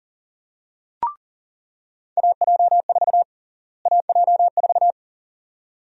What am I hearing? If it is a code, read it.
Morse code, heard as a keyed tone, about 40 wpm, 700 Hz. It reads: AJ4 AJ4
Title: USA Call Sign Prefixes - Repeated 2x - 40wpm